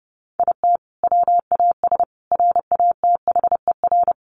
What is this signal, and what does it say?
Morse code, heard as a keyed tone, about 30 wpm, 700 Hz. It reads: IT WAS RATHER
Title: Sentences from Top 1400 Words 30wpm